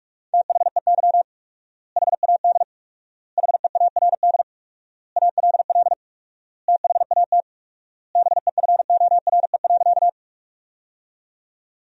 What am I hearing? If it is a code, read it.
Morse code, heard as a keyed tone, about 46 wpm, 700 Hz. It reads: THEY HAD HEARD ALL THAT BEFORE.
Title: 4x Word-Spacing - Sentences from Top 400 Words - 45wpm